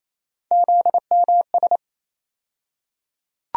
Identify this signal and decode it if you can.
Morse code, heard as a keyed tone, about 28 wpm, 700 Hz. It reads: ZMS
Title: USA Call Sign Postfixes - 3 Letters 28wpm